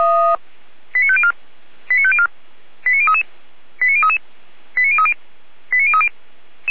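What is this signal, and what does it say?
Five-tone selective call (ZVEI), heard as quick runs of five tones, tones of about 70 ms. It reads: 89504 89504 89033 89033 89030 89030
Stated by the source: ZVEI five-tone selective call received via SDR, from the Feuerwehr Orlamünde audio server.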